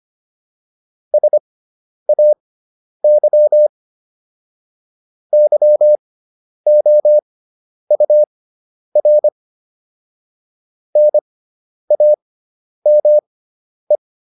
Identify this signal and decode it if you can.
Morse code, heard as a keyed tone, about 25 wpm, spaced as for 10 wpm, 600 Hz. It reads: SAY YOUR NAME